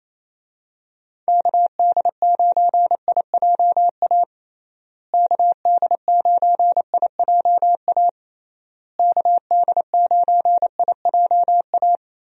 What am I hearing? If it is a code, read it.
Morse code, heard as a keyed tone, about 28 wpm, 700 Hz. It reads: KD9IJA KD9IJA KD9IJA
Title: USA Call Signs - 6 Characters - Repeated 3x - 28wpm